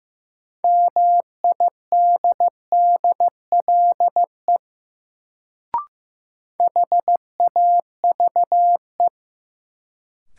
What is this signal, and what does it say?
Morse code, heard as a keyed tone, about 15 wpm, 700 Hz. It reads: MIDDLE HAVE
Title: Sets of 2 Words - New Word Drill - 15wpm